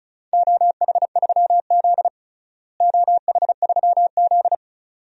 Morse code, heard as a keyed tone, about 34 wpm, 700 Hz. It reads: OH3Z OH3Z
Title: DXCC-100 Call Signs - 4 Characters - Repeated 2x - 35wpm